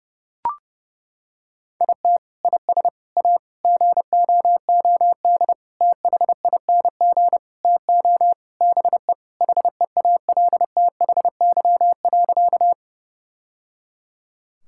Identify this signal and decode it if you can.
Morse code, heard as a keyed tone, about 32 wpm, 700 Hz. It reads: IT IS A GOOD THING TO BE HEALTHY.
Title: Sentences from Top 1000 Words - Review Part 5 - 30wpm